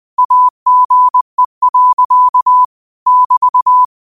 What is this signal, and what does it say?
Morse code, heard as a keyed tone, about 20 wpm, 1000 Hz. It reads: AGE. <BT>
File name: qotdfiles / 20wpm